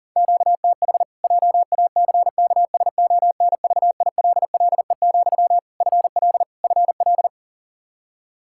Morse code, heard as a keyed tone, about 40 wpm, 700 Hz. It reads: QTH JACKSONVILLE, FL FL
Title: Practice QSO - Exchange 1 - Beginning - 40wpm